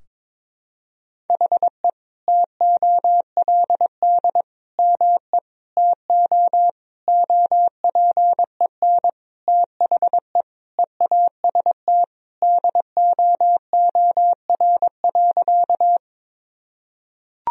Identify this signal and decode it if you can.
Morse code, heard as a keyed tone, about 22 wpm, 700 Hz. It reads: HE TOLD ME TO OPEN THE EAST DOOR.
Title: Sentences from Top 1000 Words - Review Part 3 - 22wpm